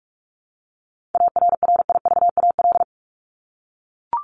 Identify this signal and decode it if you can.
Morse code, heard as a keyed tone, about 44 wpm, 700 Hz. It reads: ARRIVAL
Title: Top 1751-2000 Words 45wpm